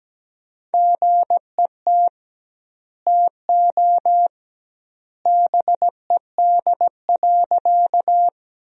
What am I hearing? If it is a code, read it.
Morse code, heard as a keyed tone, about 17 wpm, 700 Hz. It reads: GET TO BED.
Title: 2x Word-Spacing - Sentences from Top 500 Words - 17wpm